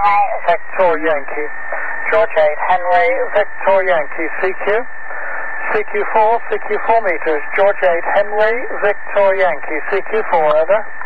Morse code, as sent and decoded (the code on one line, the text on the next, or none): none